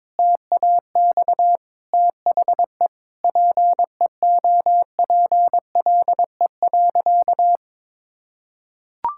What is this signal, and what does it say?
Morse code, heard as a keyed tone, about 21 wpm, 700 Hz. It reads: TAX THE PEOPLE.
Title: Sentences from Top 1000 Words - Review Part 6 - 22wpm